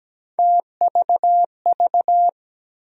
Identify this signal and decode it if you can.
Morse code, heard as a keyed tone, about 17 wpm, 700 Hz. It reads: TVV
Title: USA Call Sign Postfixes - 3 Letters 17wpm